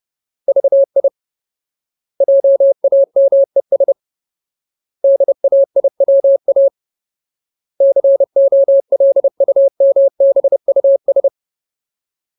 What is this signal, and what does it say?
Morse code, heard as a keyed tone, about 30 wpm, 550 Hz. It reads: VI JAMES DAIWA COLUMBUS
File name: Ultimate Word List / 30WPM Ultimate